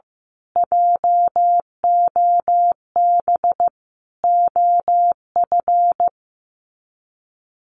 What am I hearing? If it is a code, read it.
Morse code, heard as a keyed tone, about 15 wpm, 700 Hz. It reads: JOB OF
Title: Sets of 5 Words Encore2 15wpm